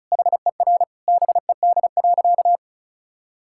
Morse code, heard as a keyed tone, about 36 wpm, 700 Hz. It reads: HER BED.
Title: Sentences from Top 1000 Words - Review Part 6 - 35wpm